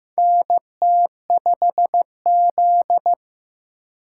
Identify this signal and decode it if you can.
Morse code, heard as a keyed tone, about 15 wpm, 700 Hz. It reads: NT5Z
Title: USA Call Signs - 4 Characters - 3x Character Spacing - 15wpm